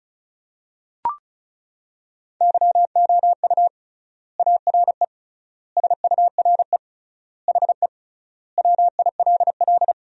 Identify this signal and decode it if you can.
Morse code, heard as a keyed tone, about 36 wpm, 700 Hz. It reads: YOU ARE SURE HE WILL
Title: 3x Word-Spacing - Sentences from Top 300 Words - 35wpm